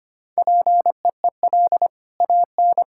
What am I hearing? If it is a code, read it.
Morse code, heard as a keyed tone, about 25 wpm, 700 Hz. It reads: PEEL AN
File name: Easy Sentences with Hints - 25wpm